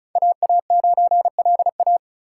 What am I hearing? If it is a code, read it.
Morse code, heard as a keyed tone, about 47 wpm, spaced as for 41 wpm, 700 Hz. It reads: AA9LA
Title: Speed Racing - USA 2x2 Call Signs 35wpm